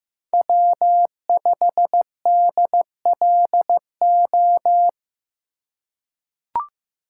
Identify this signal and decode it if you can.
Morse code, heard as a keyed tone, about 15 wpm, 700 Hz. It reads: W5DLO
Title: USA Call Signs - 5 Characters - 2x Character Spacing - 15wpm